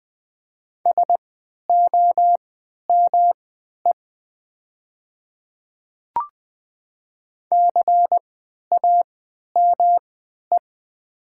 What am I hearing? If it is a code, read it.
Morse code, heard as a keyed tone, about 20 wpm, spaced as for 11 wpm, 690 Hz. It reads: SOME CAME
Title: 3x Character Spacing - Top 300 Words - 20wpm